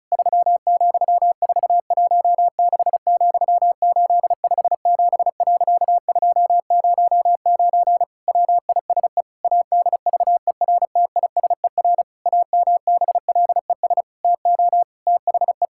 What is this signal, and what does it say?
Morse code, heard as a keyed tone, about 36 wpm, 700 Hz. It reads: 3,416,857.209 WISE ADVERTISER AMBLES TO THE
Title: Pangram Challenge v2 - 35wpm